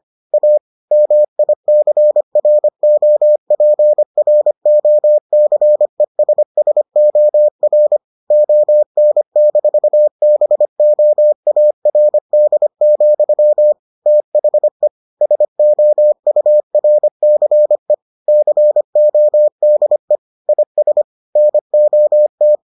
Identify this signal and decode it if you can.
Morse code, heard as a keyed tone, about 25 wpm, 600 Hz. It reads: A MICROPROCESSOR ON-BOARD, THE SOURCE CODE IS NOT